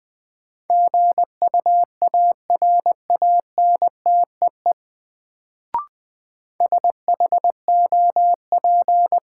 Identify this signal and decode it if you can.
Morse code, heard as a keyed tone, about 20 wpm, 700 Hz. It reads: GUARANTEE SHOP
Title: Sentences from Top 1600 Words - New Word Drill - 20wpm